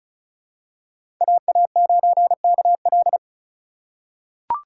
Morse code, heard as a keyed tone, about 34 wpm, 700 Hz. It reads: AA9KL
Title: USA Call Signs - 5 Characters - 4x Character Spacing - 35wpm